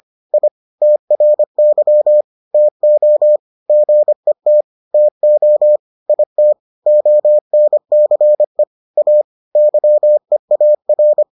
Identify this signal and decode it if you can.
Morse code, heard as a keyed tone, about 25 wpm, 600 Hz. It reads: I TRY TO GET TO IT ONCE A YEAR